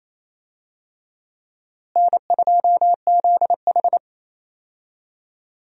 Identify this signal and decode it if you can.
Morse code, heard as a keyed tone, about 28 wpm, 700 Hz. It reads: N2ZH
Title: USA Call Signs - 4 Characters - Repeated 3x - 28wpm